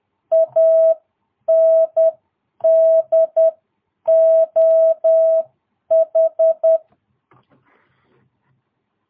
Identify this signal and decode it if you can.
Morse code, heard as a keyed tone, about 10 wpm, spaced as for 8 wpm, 660 Hz. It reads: ANDOH
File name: jh8cbh / 2022cw